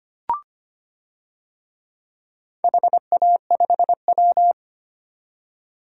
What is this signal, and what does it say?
Morse code, heard as a keyed tone, about 25 wpm, 700 Hz. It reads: HA5W
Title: DXCC-100 Call Signs - 4 Characters - 25wpm